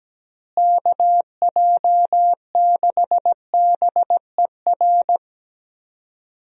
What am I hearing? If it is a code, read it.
Morse code, heard as a keyed tone, about 17 wpm, 700 Hz. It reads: KJ6BER
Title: USA Call Signs - 6 Characters - Repeated 3x - 17wpm